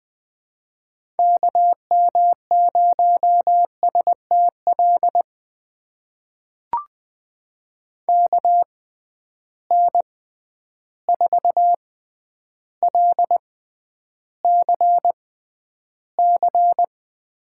Morse code, heard as a keyed tone, about 20 wpm, spaced as for 10 wpm, 700 Hz. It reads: KM0STL KN4LCC
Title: USA Call Signs - 6x Character Spacing - 20wpm